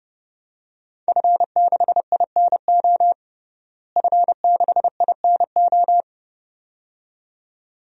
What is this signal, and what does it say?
Morse code, heard as a keyed tone, about 30 wpm, 700 Hz. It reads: F6INO F6INO
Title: DXCC-100 Call Signs - 5 Characters - Repeated 2x - 30wpm